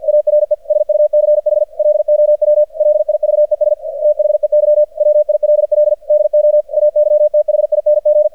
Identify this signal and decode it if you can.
Morse code, heard as a keyed tone, about 53 wpm, 600 Hz, with radio noise. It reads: ARE SUCH LOW LIVES THEY WILL DO AYTHING
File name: archived / CRSnet